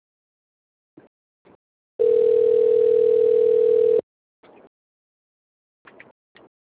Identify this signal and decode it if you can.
Ringback tone (North America), 440 + 480 Hz.